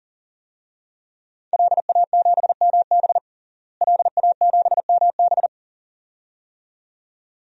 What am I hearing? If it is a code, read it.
Morse code, heard as a keyed tone, about 37 wpm, 700 Hz. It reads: LA7MB LA7MB
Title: DXCC-100 Call Signs - 5 Characters - Repeated 2x - 40wpm